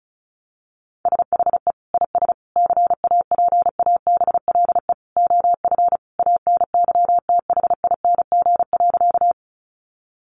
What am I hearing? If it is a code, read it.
Morse code, heard as a keyed tone, about 34 wpm, 700 Hz. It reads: SHE IS CAPABLE OF ANYTHING.
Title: Sentences from Top 1200 Words 35wpm